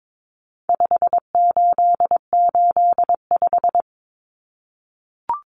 Morse code, heard as a keyed tone, about 22 wpm, 700 Hz. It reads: 5885
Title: Numbers - 4-Digit 22wpm